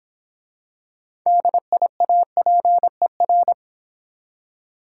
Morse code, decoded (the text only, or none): DIAPER